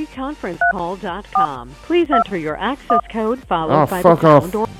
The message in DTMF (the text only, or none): 3421